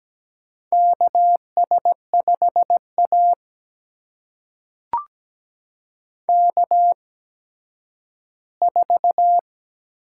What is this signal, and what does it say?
Morse code, heard as a keyed tone, about 17 wpm, spaced as for 10 wpm, 700 Hz. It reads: KS5A K4
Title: USA Call Signs - 4 Characters - 8x Character Spacing - 17wpm